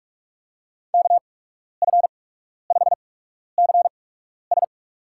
Morse code, heard as a keyed tone, about 44 wpm, spaced as for 12 wpm, 700 Hz. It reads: KF5CS
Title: USA Call Signs - 5 Characters - 8x Character Spacing - 45wpm